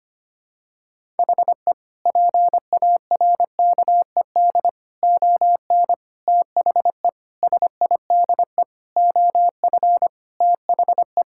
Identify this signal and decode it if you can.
Morse code, heard as a keyed tone, about 25 wpm, 700 Hz. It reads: HE PARKED ON THE SIDE OF THE
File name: practice / 25wpm / Sentences with 4-Word Phrases - 25wpm